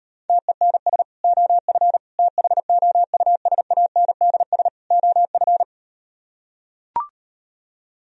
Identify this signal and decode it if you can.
Morse code, heard as a keyed tone, about 38 wpm, 700 Hz. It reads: TENS OF THOUSANDS OF